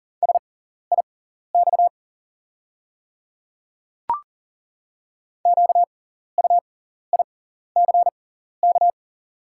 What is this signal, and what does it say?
Morse code, heard as a keyed tone, about 37 wpm, spaced as for 14 wpm, 690 Hz. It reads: SIX QUICK
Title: 6x Character Spacing - Top 500 Words - 40wpm